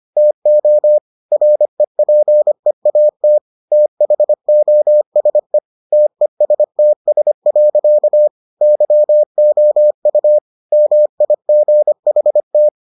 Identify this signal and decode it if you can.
Morse code, heard as a keyed tone, about 25 wpm, 600 Hz. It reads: TO REPEAT THOSE TESTS. YOU MIGHT